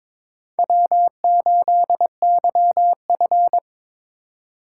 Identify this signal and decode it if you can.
Morse code, heard as a keyed tone, about 22 wpm, 700 Hz. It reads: W8YF
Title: USA Call Signs - 4 Characters - 3x Character Spacing - 22wpm